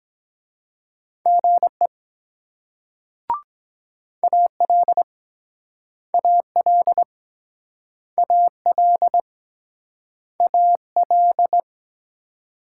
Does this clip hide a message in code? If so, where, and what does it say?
Yes: Morse code from 1.3 to 11.6 s, heard as a keyed tone, about 23 wpm, 700 Hz. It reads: GE AL AL AL AL